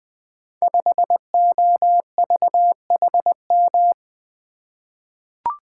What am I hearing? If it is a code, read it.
Morse code, heard as a keyed tone, about 21 wpm, 700 Hz. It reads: 5OVHM
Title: Code Groups - 5 Letter:Numbers - 20wpm